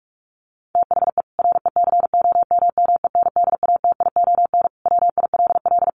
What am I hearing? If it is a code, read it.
Morse code, heard as a keyed tone, about 44 wpm, 700 Hz. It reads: THE RECOMMENDATION WILL
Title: Sentences from Top 1400 Words 45wpm